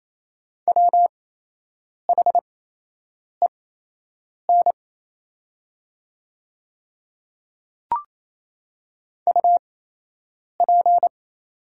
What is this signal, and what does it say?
Morse code, heard as a keyed tone, about 28 wpm, spaced as for 8 wpm, 700 Hz. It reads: WHEN UP
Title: 8x Character Spacing - Top 100 Words - 28wpm